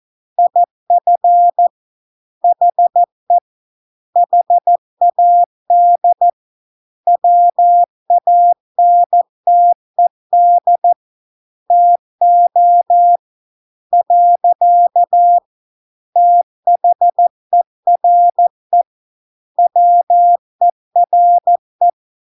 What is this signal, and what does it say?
Morse code, heard as a keyed tone, about 14 wpm, 700 Hz. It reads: IF HE HAD WANTED TO . THERE WERE